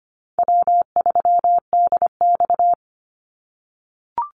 Morse code, heard as a keyed tone, about 25 wpm, 700 Hz. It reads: W3DX
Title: NA SOTA Activator Callsigns 25wpm